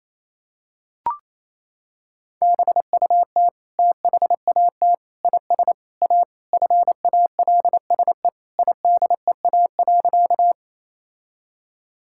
Morse code, heard as a keyed tone, about 27 wpm, 700 Hz. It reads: BUT THAT IS A FALSE IDEA.
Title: Sentences from Top 1600 Words 28wpm